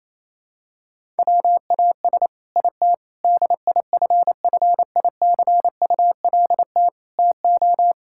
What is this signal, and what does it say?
Morse code, heard as a keyed tone, about 28 wpm, 700 Hz. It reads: WAS IT DIFFICULT TO
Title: Sentences from Top 1000 Words - Review Part 6 - 28wpm